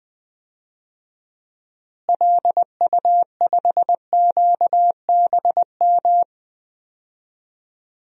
Morse code, heard as a keyed tone, about 20 wpm, 700 Hz. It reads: LU5QBM
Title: DXCC-100 Call Signs - 6 Characters - Repeated 2x - 20wpm